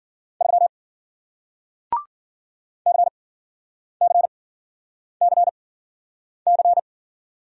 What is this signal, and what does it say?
Morse code, heard as a keyed tone, about 56 wpm, spaced as for 9 wpm, 700 Hz. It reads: 3 CCCC